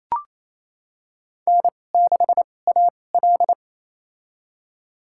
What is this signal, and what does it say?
Morse code, heard as a keyed tone, about 28 wpm, spaced as for 20 wpm, 700 Hz. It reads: N6AL
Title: USA Call Signs - 2x Character Spacing - 28wpm